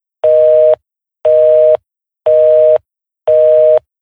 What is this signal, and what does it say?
Busy tone (North America), 480 + 620 Hz, with noise.
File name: Occupied